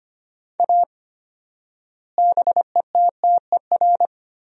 Morse code, heard as a keyed tone, about 25 wpm, 700 Hz. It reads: A BETTER
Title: 4x Word-Spacing - Sentences from Top 400 Words - 25wpm